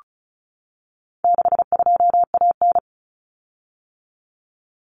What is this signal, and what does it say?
Morse code, heard as a keyed tone, about 34 wpm, 700 Hz. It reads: 62AN
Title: Common Cut Numbers - 4-Digit 35wpm